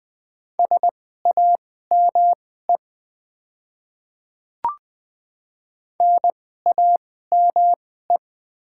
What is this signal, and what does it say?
Morse code, heard as a keyed tone, about 20 wpm, spaced as for 14 wpm, 700 Hz. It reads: SAME NAME
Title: 2x Character Spacing - Top 200 Words - 20wpm